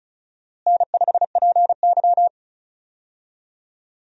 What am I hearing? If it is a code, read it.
Morse code, heard as a keyed tone, about 36 wpm, 700 Hz. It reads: N5PY
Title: USA Call Signs - 4 Characters - Repeated 3x - 35wpm